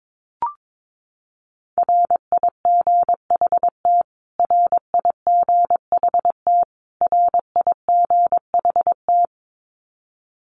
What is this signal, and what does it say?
Morse code, heard as a keyed tone, about 22 wpm, 700 Hz. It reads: RIGHT RIGHT RIGHT
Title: Top 500 Words Repeated 22wpm